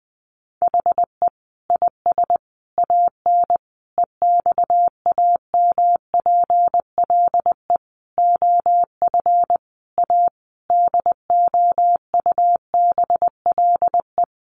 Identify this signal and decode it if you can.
Morse code, heard as a keyed tone, about 20 wpm, 700 Hz. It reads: HE IS AN EXAMPLE OF A DOUBLE
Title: Sentences from Top 1500 Words 20wpm